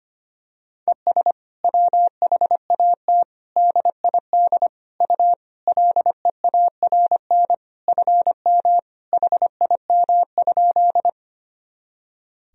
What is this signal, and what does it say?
Morse code, heard as a keyed tone, about 25 wpm, 700 Hz. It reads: ES WHAT DID U LEARN FM HIM?
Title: Sentences from Top 300 Words and Abbreviations 25wpm